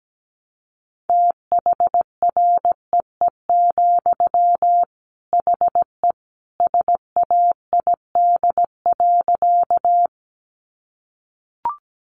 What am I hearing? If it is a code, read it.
Morse code, heard as a keyed tone, about 17 wpm, 700 Hz. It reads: THREE, HE SAID.
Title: Sentences from Top 200 Words 17wpm